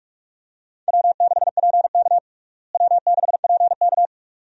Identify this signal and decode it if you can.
Morse code, heard as a keyed tone, about 46 wpm, 700 Hz. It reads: W6PK W6PK
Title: USA Call Signs - 4 Characters - Repeated 2x - 45wpm